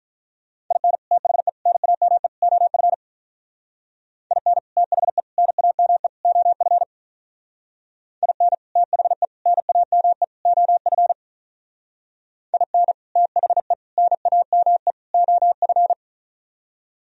Morse code, heard as a keyed tone, about 42 wpm, 700 Hz. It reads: IN THE NAME OF IN THE NAME OF IN THE NAME OF IN THE NAME OF
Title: Speed-Racing - Sets of 4 Words 35wpm